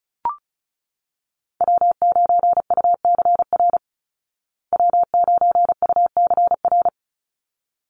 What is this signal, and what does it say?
Morse code, heard as a keyed tone, about 34 wpm, 700 Hz. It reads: W9UCR W9UCR
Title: Call Signs Repeated 3x - 35wpm